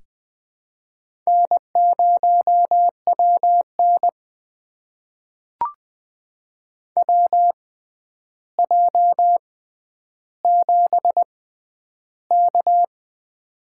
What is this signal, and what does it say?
Morse code, heard as a keyed tone, about 20 wpm, spaced as for 10 wpm, 700 Hz. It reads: N0WN WJ7K
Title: USA Call Signs - 4 Characters - 6x Character Spacing - 20wpm